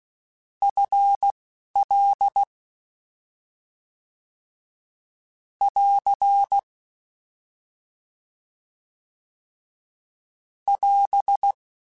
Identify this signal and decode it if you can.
Morse code, heard as a keyed tone, about 16 wpm, 800 Hz. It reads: F L <AR> <AS>
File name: Contrasts / Contrasts-mp3s / 2LtrContrasts @10